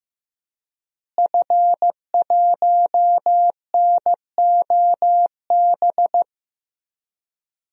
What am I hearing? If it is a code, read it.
Morse code, heard as a keyed tone, about 15 wpm, 700 Hz. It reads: F1NOB
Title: DXCC-100 Call Signs - 5 Characters - Repeated 3x - 15wpm